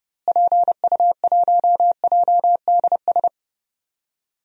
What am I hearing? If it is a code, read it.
Morse code, heard as a keyed tone, about 30 wpm, 700 Hz. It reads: PU1JDS
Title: DXCC-100 Call Signs - 6 Characters - Repeated 2x - 30wpm